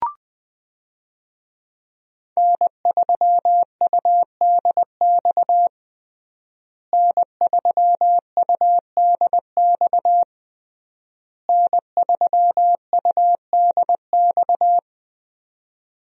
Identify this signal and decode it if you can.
Morse code, heard as a keyed tone, about 20 wpm, 700 Hz. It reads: N3UDX N3UDX N3UDX
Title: USA Call Signs - 5 Characters - Repeated 3x - 20wpm